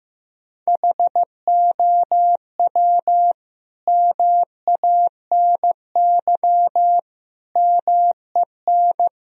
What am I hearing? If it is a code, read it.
Morse code, heard as a keyed tone, about 15 wpm, 700 Hz. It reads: HOW MANY MEN
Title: Sentences from Top 1000 Words - Review Part 3 - 15wpm